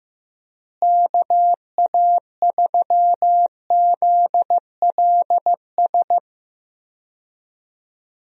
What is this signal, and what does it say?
Morse code, heard as a keyed tone, about 15 wpm, 700 Hz. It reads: KA3ZLS
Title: USA Call Signs - 6 Characters - Repeated 3x - 15wpm